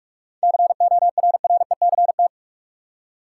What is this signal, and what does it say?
Morse code, heard as a keyed tone, about 43 wpm, 700 Hz. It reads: CORRECT
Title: Seven Letter Words - 45wpm